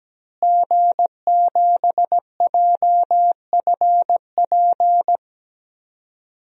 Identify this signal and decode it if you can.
Morse code, heard as a keyed tone, about 17 wpm, 700 Hz. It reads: G7JFP
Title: DXCC-100 Call Signs - 5 Characters - Repeated 2x - 17wpm